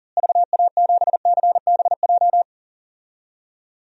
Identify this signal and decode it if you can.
Morse code, heard as a keyed tone, about 37 wpm, 700 Hz. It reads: VA7CBJ
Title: DXCC-100 Call Signs - 6 Characters - Repeated 3x - 40wpm